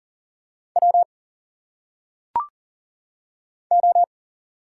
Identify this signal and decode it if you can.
Morse code, heard as a keyed tone, about 40 wpm, 700 Hz. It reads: W O
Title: Single Letters - Phonetic Pronunciation 40wpm